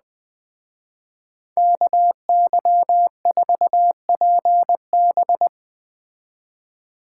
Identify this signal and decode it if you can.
Morse code, heard as a keyed tone, about 20 wpm, 700 Hz. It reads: KY4PB
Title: Speed Racing - USA 2x2 Call Signs 17wpm